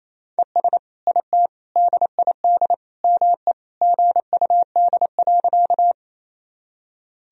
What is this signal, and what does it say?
Morse code, heard as a keyed tone, about 28 wpm, 700 Hz. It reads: ES IT DID ME GUD.
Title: Sentences from Top 200 Words and Abbreviations 28wpm